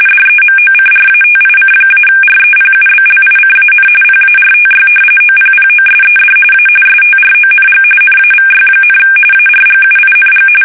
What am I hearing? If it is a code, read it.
RTTY, about 45.45 baud. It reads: DEMOCRATIC-REPUBLICAN PARTY OPPOSED A STRONG NAVY, ARGUING
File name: fsk850 / usnavy1